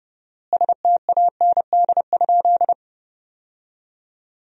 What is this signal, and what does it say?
Morse code, heard as a keyed tone, about 32 wpm, 700 Hz. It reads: STAND?
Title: 4x Word-Spacing - Sentences from Top 300 Words - 30wpm